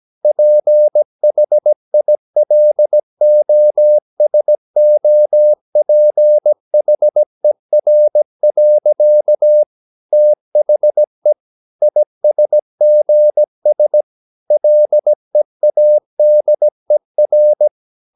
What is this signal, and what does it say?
Morse code, heard as a keyed tone, about 17 wpm, 600 Hz. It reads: PHILOSOPHER. THE ISGS LEADER
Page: Romanen Never af Ken Follet på engelsk.